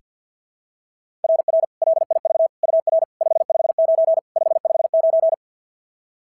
Morse code, heard as a keyed tone, about 45 wpm, 660 Hz. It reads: RR LIV UR 559 559
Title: SOTA QSO Practice 50wpm